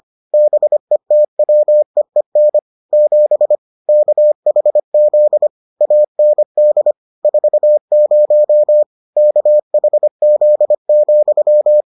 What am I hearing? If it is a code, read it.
Morse code, heard as a keyed tone, about 25 wpm, 600 Hz. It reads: BETWEEN 7 KHZ AND 40 KHZ,